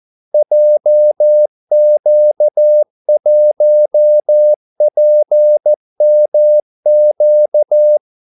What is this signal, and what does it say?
Morse code, heard as a keyed tone, about 14 wpm, 600 Hz. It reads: JQ1PMQ
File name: Callsigns-14wpm